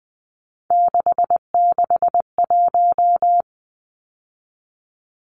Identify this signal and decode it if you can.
Morse code, heard as a keyed tone, about 20 wpm, 700 Hz. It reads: 661
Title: Numbers - 3-Digit 20wpm